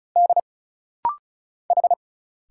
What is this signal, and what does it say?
Morse code, heard as a keyed tone, about 34 wpm, 700 Hz. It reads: D H
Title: Single Letters - Rapid-Fire 35wpm